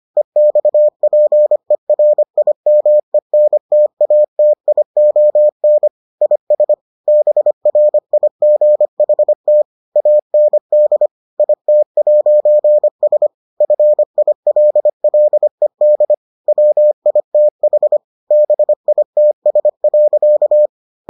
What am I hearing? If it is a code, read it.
Morse code, heard as a keyed tone, about 25 wpm, 600 Hz. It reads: EXPERIMENTATION IS BRIGHT AND IT'S FILLED WITH BITS.